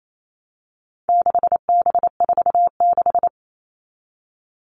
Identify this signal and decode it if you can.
Morse code, heard as a keyed tone, about 28 wpm, 700 Hz. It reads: 6B46
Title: All Cut Numbers - 4-Digit 28wpm